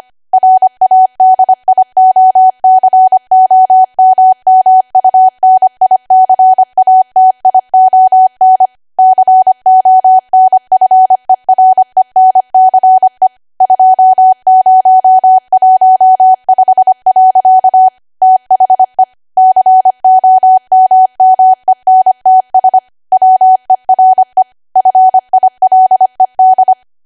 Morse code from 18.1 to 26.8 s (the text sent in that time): THE COMMENTS WERE FILED